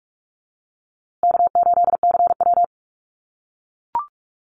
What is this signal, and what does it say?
Morse code, heard as a keyed tone, about 44 wpm, 700 Hz. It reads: K8CW
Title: USA Call Signs Repeated 3x 45wpm